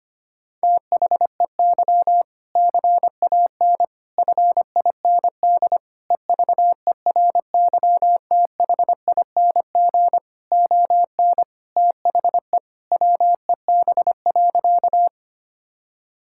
Morse code, heard as a keyed tone, about 25 wpm, 700 Hz. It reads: THEY CAN FIND EVERYTHING ON THE WEB.
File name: practice / 25wpm / All-Age Appropriate Jokes - 25wpm